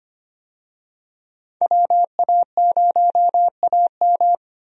Morse code, heard as a keyed tone, about 25 wpm, 700 Hz. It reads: WA0AM
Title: USA Call Signs - 5 Characters - 25wpm